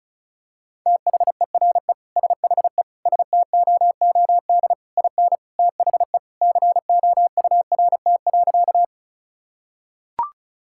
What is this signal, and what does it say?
Morse code, heard as a keyed tone, about 36 wpm, 700 Hz. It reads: THERE SHE STOOD IN THE COURT.
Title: Sentences from Top 1600 Words 35wpm